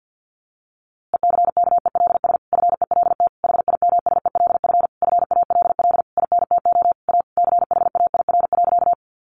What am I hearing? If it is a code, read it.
Morse code, heard as a keyed tone, about 50 wpm, 700 Hz. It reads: ECKELS FELT HIMSELF FALL INTO A CHAIR.
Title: A Sound Of Thunder 50wpm